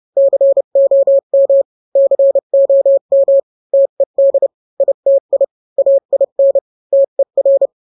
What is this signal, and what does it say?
Morse code, heard as a keyed tone, about 30 wpm, 550 Hz. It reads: COM COM TED ITI AIN TER